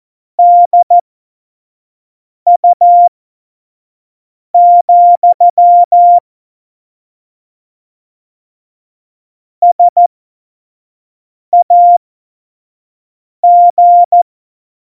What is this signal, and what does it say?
Morse code, heard as a keyed tone, about 14 wpm, spaced as for 5 wpm, 700 Hz. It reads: DU, SAG
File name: Morsekurs / Witze / Witz 14 April 2020, 25 F70